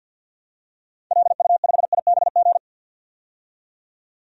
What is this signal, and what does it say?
Morse code, heard as a keyed tone, about 48 wpm, 700 Hz. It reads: LU5IBG
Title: DXCC-100 Call Signs - 6 Characters - 50wpm